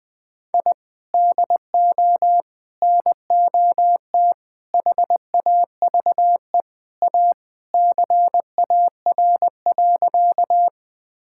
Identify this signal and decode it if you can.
Morse code, heard as a keyed tone, about 20 wpm, 700 Hz. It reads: I DO NOT HAVE A CAR.
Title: Sentences with 5-Word Phrases - 20wpm